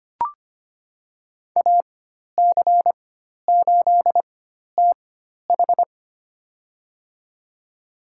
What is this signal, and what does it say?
Morse code, heard as a keyed tone, about 25 wpm, spaced as for 12 wpm, 700 Hz. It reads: AC8TH
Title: USA Call Signs - 4x Character Spacing - 25wpm